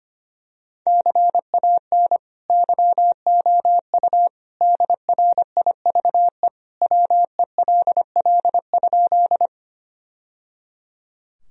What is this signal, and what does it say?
Morse code, heard as a keyed tone, about 25 wpm, 700 Hz. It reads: CAN YOU DRIVE WELL?
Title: Sentences from Top 1000 Words - Review Part 2 - 25wpm